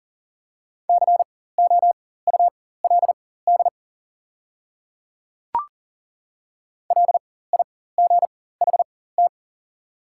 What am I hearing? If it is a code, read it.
Morse code, heard as a keyed tone, about 40 wpm, spaced as for 19 wpm, 700 Hz. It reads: COULD LIGHT
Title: 4x Character Spacing - Top 200 Words - 40wpm